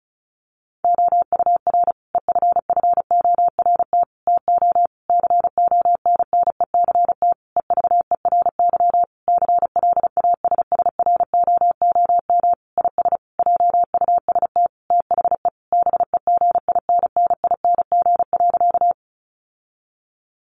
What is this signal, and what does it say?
Morse code, heard as a keyed tone, about 34 wpm, 700 Hz. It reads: OUR EFFORT TO CONNECT EVERY CLASSROOM IS JUST THE BEGINNING.
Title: Sentences from Top 1500 Words 35wpm